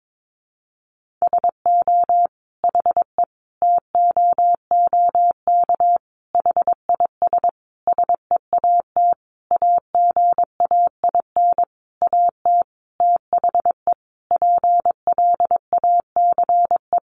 Morse code, heard as a keyed tone, about 22 wpm, 700 Hz. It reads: SO HE TOOK HIS SEAT AGAIN AT THE PLACE
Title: Sentences from Top 1500 Words 22wpm